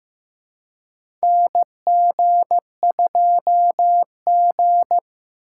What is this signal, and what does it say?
Morse code, heard as a keyed tone, about 15 wpm, 700 Hz. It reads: NG2G
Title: USA Call Signs - 4 Characters - 4x Character Spacing - 15wpm